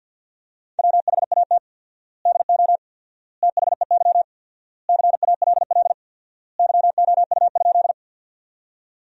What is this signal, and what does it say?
Morse code, heard as a keyed tone, about 53 wpm, 700 Hz. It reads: WHAT DO THEY CALL YOU?
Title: 4x Word-Spacing - Sentences from Top 100 Words - 50wpm